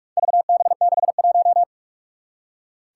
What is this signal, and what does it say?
Morse code, heard as a keyed tone, about 44 wpm, 700 Hz. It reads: VB61
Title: Four Challenging Letter-Number Combinations 45wpm